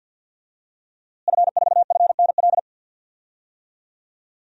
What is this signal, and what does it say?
Morse code, heard as a keyed tone, about 51 wpm, 700 Hz. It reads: F4FNL
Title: DXCC-100 Call Signs - 5 Characters - Repeated 3x - 50wpm